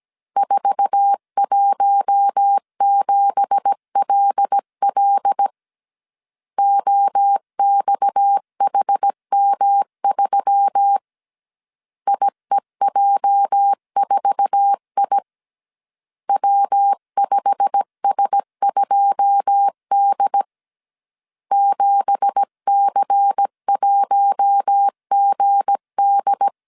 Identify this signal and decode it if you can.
Morse code, heard as a keyed tone, about 17 wpm, 800 Hz. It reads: 417LL OXHM3 IEJ4I W5S2D 7C1GD
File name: CW / DET / 1 9 1500C S A 16wpm 2828